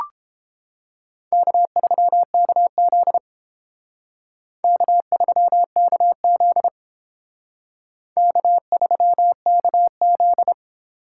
Morse code, heard as a keyed tone, about 30 wpm, 700 Hz. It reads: K3KZ K3KZ K3KZ